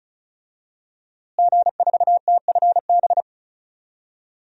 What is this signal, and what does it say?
Morse code, heard as a keyed tone, about 34 wpm, 700 Hz. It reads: G4TFB
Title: DXCC-100 Call Signs - 5 Characters - Repeated 3x - 35wpm